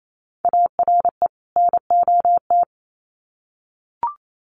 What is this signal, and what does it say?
Morse code, heard as a keyed tone, about 28 wpm, 700 Hz. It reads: ARE NOT
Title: Sets of 2 Words 28wpm